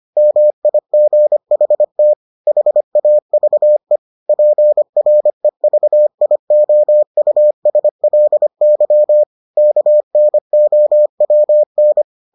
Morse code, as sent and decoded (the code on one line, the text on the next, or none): -- .. --. .... - / .... .- ...- . / .--. .-. ...- .. --- ..- ... .-.. -.-- / -.- -. --- .-- -.
MIGHT HAVE PREVIOUSLY KNOWN